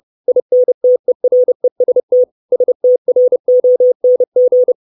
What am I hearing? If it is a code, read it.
Morse code, heard as a keyed tone, about 30 wpm, 500 Hz. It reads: INTEREST STRONG
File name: 632 Words 30WPM